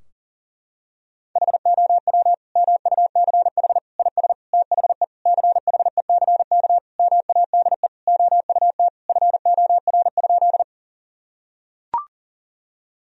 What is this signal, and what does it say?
Morse code, heard as a keyed tone, about 40 wpm, 700 Hz. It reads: HOW MUCH IS THE CHECK MADE OUT FOR?
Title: Sentences from Top 1000 Words - Review Part 3 - 40wpm